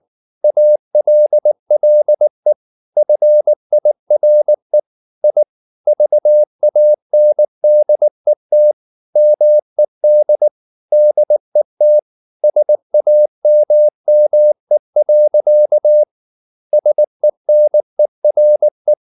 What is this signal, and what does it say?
Morse code, heard as a keyed tone, about 19 wpm, 600 Hz. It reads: ALLE FIRE I VANDET MED DET SAMME. SENERE